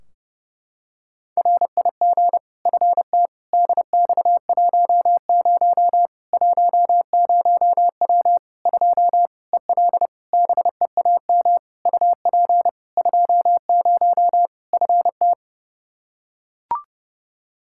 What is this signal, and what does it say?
Morse code, heard as a keyed tone, about 30 wpm, 700 Hz. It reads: RIG FT DX10 10W 2 EL BEAM UP 20 FT
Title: Practice QSO - Exchange 2 - Info - 30wpm